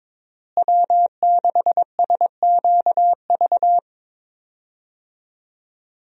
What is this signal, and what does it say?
Morse code, heard as a keyed tone, about 21 wpm, 700 Hz. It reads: W6SQV
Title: USA Call Signs - 5 Characters - Repeated 3x - 22wpm